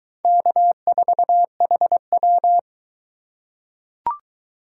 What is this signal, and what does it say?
Morse code, heard as a keyed tone, about 22 wpm, 700 Hz. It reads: K4HW